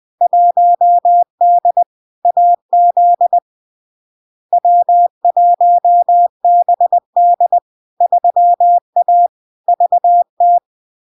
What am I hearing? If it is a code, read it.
Morse code, heard as a keyed tone, about 20 wpm, 700 Hz. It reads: 1D AZ W1BD 3A VT
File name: LARK-FD-2022-Log-in-CW-20wpm-700-Hz